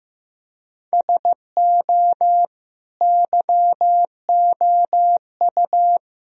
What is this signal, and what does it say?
Morse code, heard as a keyed tone, about 16 wpm, 700 Hz. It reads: SO YOU
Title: Sentences from Top 1000 Words - Review Part 4 - 15wpm